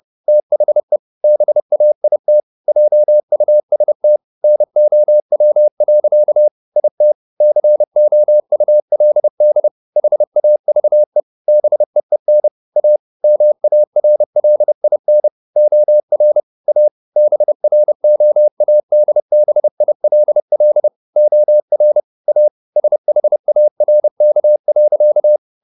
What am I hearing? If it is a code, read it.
Morse code, heard as a keyed tone, about 30 wpm, 600 Hz. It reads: THE BAIT JUST NOW. IT COULD HAVE BEEN A MARLIN OR A BROADBILL OR A SHARK.
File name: the old man and the sea / the old man and the sea 1 30wpm